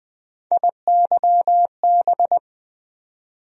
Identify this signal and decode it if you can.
Morse code, heard as a keyed tone, about 20 wpm, 700 Hz. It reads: IYB
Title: USA Call Sign Postfixes - 3 Letters 20wpm